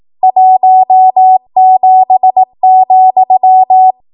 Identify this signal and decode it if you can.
Morse code, heard as a keyed tone, about 18 wpm, 750 Hz. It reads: 17,
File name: Archive / 18 WPM / 18WPM